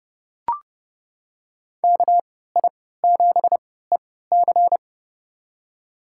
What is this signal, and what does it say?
Morse code, heard as a keyed tone, about 30 wpm, spaced as for 17 wpm, 700 Hz. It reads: KI7EC
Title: USA Call Signs - 5 Characters - 3x Character Spacing - 30wpm